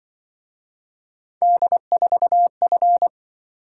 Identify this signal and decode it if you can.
Morse code, heard as a keyed tone, about 24 wpm, 700 Hz. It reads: D4F
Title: Speed-Racing - Three Letter-Number Combinations - 20wpm